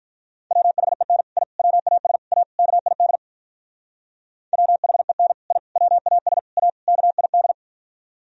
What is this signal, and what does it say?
Morse code, heard as a keyed tone, about 46 wpm, 700 Hz. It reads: WHEN I WAS A KID WHEN I WAS A KID